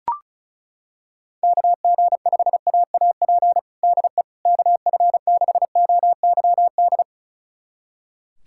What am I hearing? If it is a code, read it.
Morse code, heard as a keyed tone, about 36 wpm, 700 Hz. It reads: KG5AAP DE KF6OYD
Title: USA Callsign DE Callsign - 6 Characters - 35wpm